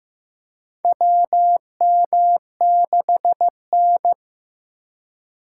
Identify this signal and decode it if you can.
Morse code, heard as a keyed tone, about 15 wpm, 700 Hz. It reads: WM6N
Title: USA Call Signs - 4 Characters - 3x Character Spacing - 15wpm